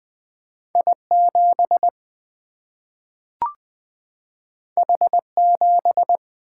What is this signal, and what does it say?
Morse code, heard as a keyed tone, about 20 wpm, 700 Hz. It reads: I 7 H 7
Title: Single-Letter Single-Number 20wpm